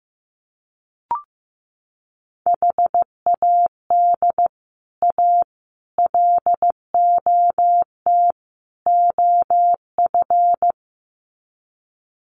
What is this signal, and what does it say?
Morse code, heard as a keyed tone, about 15 wpm, 700 Hz. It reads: HAD A LOT OF
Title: Sets of 4 Words 15wpm